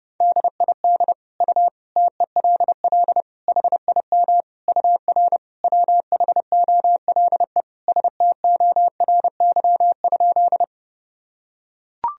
Morse code, heard as a keyed tone, about 29 wpm, 700 Hz. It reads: DID U TELL HIM UR WHOLE STORY?
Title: Sentences from Top 400 Words and Abbreviations 30wpm